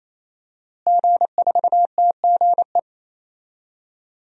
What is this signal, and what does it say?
Morse code, heard as a keyed tone, about 28 wpm, 700 Hz. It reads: G4TGE
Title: DXCC-100 Call Signs - 5 Characters - Repeated 3x - 28wpm